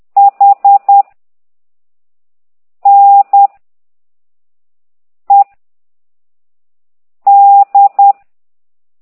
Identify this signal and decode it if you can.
Morse code, heard as a keyed tone, about 10 wpm, 800 Hz. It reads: H N E D